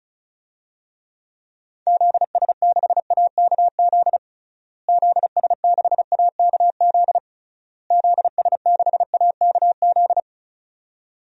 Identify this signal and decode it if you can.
Morse code, heard as a keyed tone, about 34 wpm, 700 Hz. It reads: ZS6AKZ ZS6AKZ ZS6AKZ
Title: DXCC-100 Call Signs - 6 Characters - Repeated 3x - 35wpm